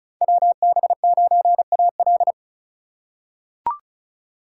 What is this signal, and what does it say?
Morse code, heard as a keyed tone, about 34 wpm, 690 Hz. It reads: WB9AL
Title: USA Call Signs - 5 Characters - 2x Character Spacing - 35wpm